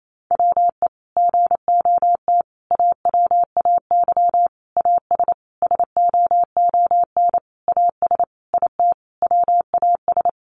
Morse code, heard as a keyed tone, about 28 wpm, 700 Hz. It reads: WE GOT AWAY AS SOON AS IT WAS
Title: Sentences from Top 600 Words 28wpm